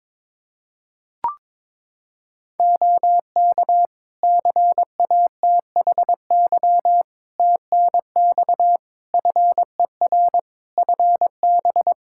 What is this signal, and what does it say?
Morse code, heard as a keyed tone, about 22 wpm, 700 Hz. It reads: OK CATHY TNX FER FB
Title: Practice QSO - Exchange 4 - Ending - 22wpm